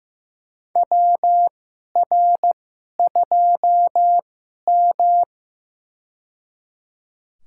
Morse code, heard as a keyed tone, about 15 wpm, spaced as for 11 wpm, 700 Hz. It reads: WR2M
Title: USA Call Signs - 4 Characters - 2x Character Spacing - 15wpm